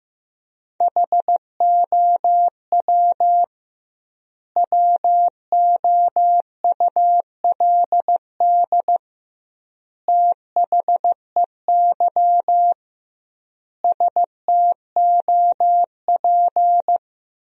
Morse code, heard as a keyed tone, about 16 wpm, 700 Hz. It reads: HOW WOULD THEY STOP
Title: 2x Word-Spacing - Sentences from Top 300 Words - 15wpm